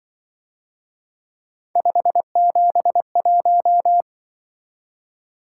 Morse code, heard as a keyed tone, about 24 wpm, 700 Hz. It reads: 571
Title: Speed-Racing - Three Letter-Number Combinations - 20wpm